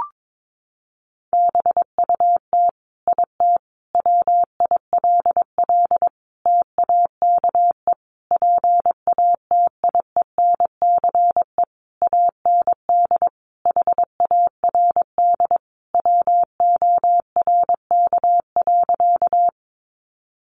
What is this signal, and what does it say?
Morse code, heard as a keyed tone, about 22 wpm, 700 Hz. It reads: BUT IT WILL TAKE PATIENCE AND HARD WORK.
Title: Sentences from Top 1300 Words 22wpm